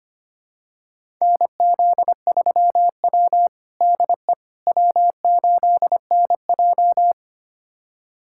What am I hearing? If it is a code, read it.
Morse code, heard as a keyed tone, about 25 wpm, 700 Hz. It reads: NZ3W DE W8NJ
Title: USA Callsign DE Callsign - 4 Characters - 25wpm